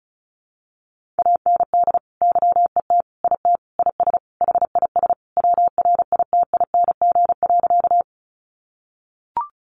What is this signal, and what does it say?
Morse code, heard as a keyed tone, about 34 wpm, 700 Hz. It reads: AND YET IT IS HIS WRITING.
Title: Sentences from Top 800 Words 35wpm